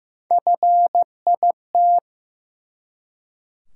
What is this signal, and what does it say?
Morse code, heard as a keyed tone, about 15 wpm, 700 Hz. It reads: FIT
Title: Three Letter Words - 15wpm